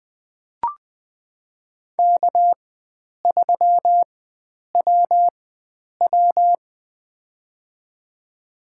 Morse code, heard as a keyed tone, about 21 wpm, spaced as for 9 wpm, 700 Hz. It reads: K3WW
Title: USA Call Signs - 4 Characters - 4x Character Spacing - 20wpm